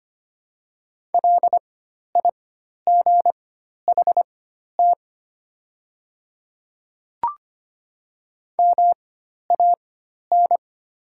Morse code, heard as a keyed tone, about 25 wpm, spaced as for 12 wpm, 700 Hz. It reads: LIGHT MAN
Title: 4x Character Spacing - Top 200 Words - 25wpm